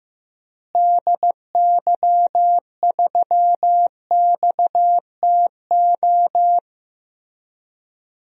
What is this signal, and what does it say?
Morse code, heard as a keyed tone, about 15 wpm, 700 Hz. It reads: DY3XTO
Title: DXCC-100 Call Signs - 6 Characters - Repeated 3x - 15wpm